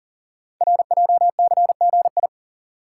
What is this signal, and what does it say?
Morse code, heard as a keyed tone, about 44 wpm, 700 Hz. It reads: RJCGI
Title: Code Groups - 5 Letters - 40wpm